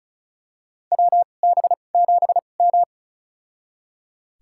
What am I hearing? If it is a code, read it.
Morse code, heard as a keyed tone, about 36 wpm, spaced as for 25 wpm, 700 Hz. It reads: WB7M
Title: USA Call Signs - 4 Characters - 2x Character Spacing - 35wpm